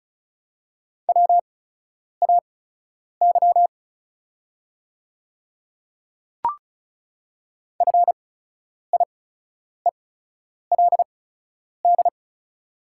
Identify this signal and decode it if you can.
Morse code, heard as a keyed tone, about 36 wpm, spaced as for 10 wpm, 700 Hz. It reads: WAY FIELD
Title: 8x Character Spacing - Top 500 Words - 35wpm